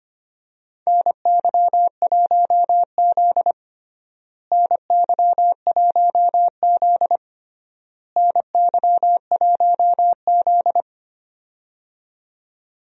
Morse code, heard as a keyed tone, about 25 wpm, 700 Hz. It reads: NY1Z NY1Z NY1Z
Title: USA Call Signs - 4 Characters - Repeated 3x - 25wpm